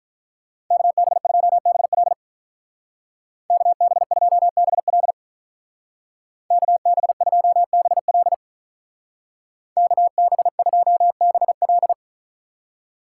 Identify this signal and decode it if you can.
Morse code, heard as a keyed tone, about 44 wpm, 700 Hz. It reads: KB2BL KB2BL KB2BL KB2BL
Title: Speed Racing - USA Call Signs 35wpm